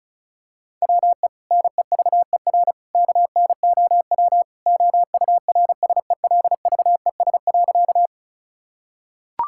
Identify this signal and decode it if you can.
Morse code, heard as a keyed tone, about 36 wpm, 700 Hz. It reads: WE NEVER KNOW OURSELVES.
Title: Sentences from Top 1600 Words 35wpm